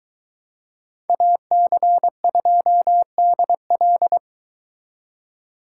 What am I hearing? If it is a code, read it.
Morse code, heard as a keyed tone, about 23 wpm, 700 Hz. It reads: AC2DL